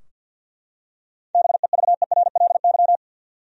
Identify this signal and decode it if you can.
Morse code, heard as a keyed tone, about 52 wpm, 700 Hz. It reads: BEVERLY
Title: USA Common Names - 50wpm